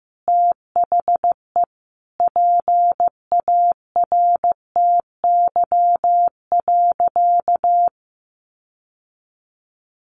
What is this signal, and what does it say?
Morse code, heard as a keyed tone, about 15 wpm, 700 Hz. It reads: THE PARTY.
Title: Sentences from Top 1400 Words 15wpm